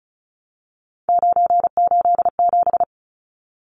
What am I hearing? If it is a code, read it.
Morse code, heard as a keyed tone, about 34 wpm, 700 Hz. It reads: 987
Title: Numbers - 3-Digit 35wpm